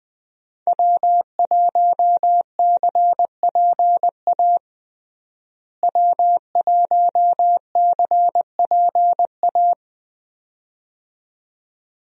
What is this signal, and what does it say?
Morse code, heard as a keyed tone, about 20 wpm, 700 Hz. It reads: W1CPA W1CPA
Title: USA Call Signs - 5 Characters - Repeated 2x - 20wpm